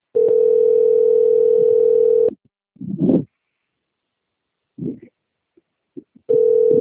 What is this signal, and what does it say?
Ringback tone (North America), 440 + 480 Hz.